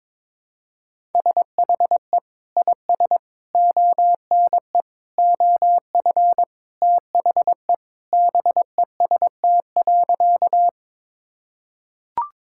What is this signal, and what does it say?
Morse code, heard as a keyed tone, about 22 wpm, 700 Hz. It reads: SHE IS ONE OF THE BEST.
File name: practice / 22wpm / Sentences with 2-Word Phrases - 22wpm